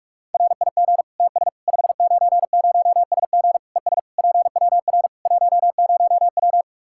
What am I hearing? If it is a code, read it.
Morse code, heard as a keyed tone, about 42 wpm, 700 Hz. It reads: RIG TS 590SG ES PWR 10W
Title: Practice QSO - Exchange 2 - Info - 45wpm